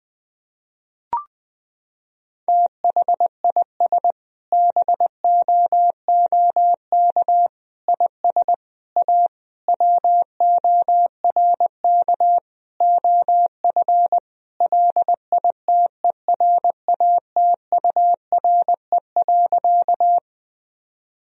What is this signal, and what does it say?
Morse code, heard as a keyed tone, about 19 wpm, 700 Hz. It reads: THIS BOOK IS A WORK OF LITERATURE.
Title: Sentences from Top 1000 Words - Review Part 3 - 20wpm